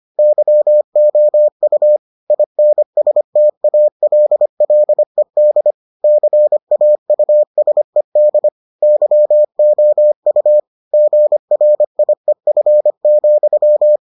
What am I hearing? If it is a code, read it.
Morse code, heard as a keyed tone, about 25 wpm, 600 Hz. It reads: YOU INSTALLED CAUSED YOU GRIEF,